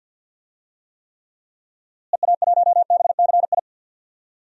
Morse code, heard as a keyed tone, about 51 wpm, 700 Hz. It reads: EA1BCI
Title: DXCC-100 Call Signs - 6 Characters - Repeated 3x - 50wpm